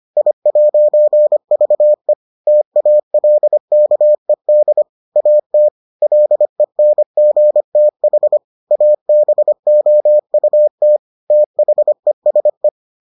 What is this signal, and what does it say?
Morse code, heard as a keyed tone, about 25 wpm, 600 Hz. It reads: I'VE TALKED AT LENGTH ABOUT THESE